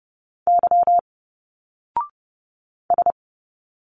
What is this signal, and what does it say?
Morse code, heard as a keyed tone, about 30 wpm, 700 Hz. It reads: Y S
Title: Single Letters 30wpm